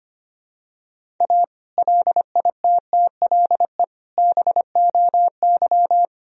Morse code, heard as a keyed tone, about 25 wpm, 700 Hz. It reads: A LITTLE BOY
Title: Sentences from Top 1000 Words - Review Part 1 - 25wpm